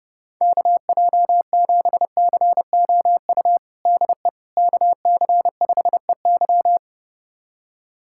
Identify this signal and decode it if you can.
Morse code, heard as a keyed tone, about 32 wpm, 700 Hz. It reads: KJ7COU DE KC5EY
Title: USA Callsign DE Callsign 30wpm